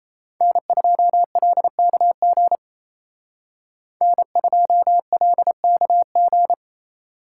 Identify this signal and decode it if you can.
Morse code, heard as a keyed tone, about 36 wpm, 700 Hz. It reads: N2LKG N2LKG